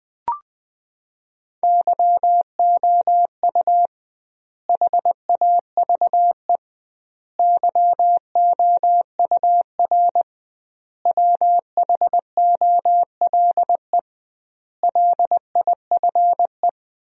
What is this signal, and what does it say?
Morse code, heard as a keyed tone, about 20 wpm, 700 Hz. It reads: YOU HAVE YOUR WHOLE LIFE
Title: 2x Word-Spacing - Sentences from Top 400 Words - 20wpm